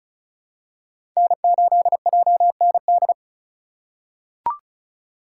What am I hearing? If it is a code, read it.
Morse code, heard as a keyed tone, about 34 wpm, 700 Hz. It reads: N8JND
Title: USA Call Signs - 2x Character Spacing - 35wpm